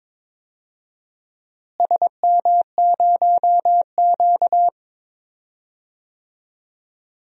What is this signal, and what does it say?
Morse code, heard as a keyed tone, about 22 wpm, 700 Hz. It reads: SM0Q
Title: DXCC-100 Call Signs - 4 Characters - Repeated 3x - 22wpm